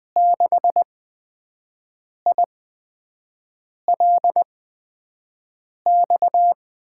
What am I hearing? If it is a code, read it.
Morse code, heard as a keyed tone, about 20 wpm, spaced as for 5 wpm, 700 Hz. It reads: 6ILX
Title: USA Call Signs - 6 Characters - 8x Character Spacing - 20wpm